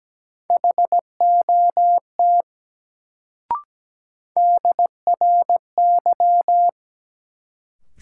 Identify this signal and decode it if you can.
Morse code, heard as a keyed tone, about 17 wpm, 700 Hz. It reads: HOT DRY
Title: Sentences from Top 500 Words - New Word Drill - 17wpm